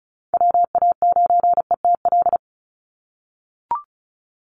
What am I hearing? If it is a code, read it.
Morse code, heard as a keyed tone, about 34 wpm, 700 Hz. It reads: WA9ETL
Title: USA Call Signs 35wpm